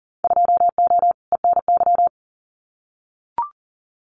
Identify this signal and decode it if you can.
Morse code, heard as a keyed tone, about 40 wpm, 700 Hz. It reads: 2O ENY
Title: Winter Field Day Exchanges Encore 40wpm